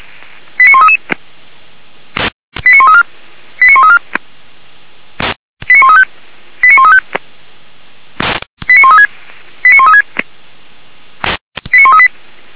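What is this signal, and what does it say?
Five-tone selective call (ZVEI), heard as quick runs of five tones, tones of about 80 ms. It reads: 89133 89134 89134 89135 89135 89136 89136 89138
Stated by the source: ZVEI five-tone call 89132, received by SDR.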